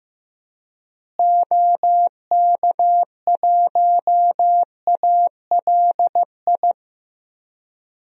Morse code, heard as a keyed tone, about 16 wpm, 700 Hz. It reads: OK1ALI
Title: DXCC-100 Call Signs - 6 Characters - Repeated 3x - 15wpm